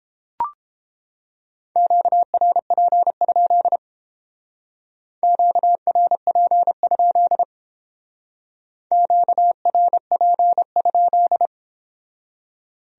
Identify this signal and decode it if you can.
Morse code, heard as a keyed tone, about 29 wpm, 700 Hz. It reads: QRP? QRP? QRP?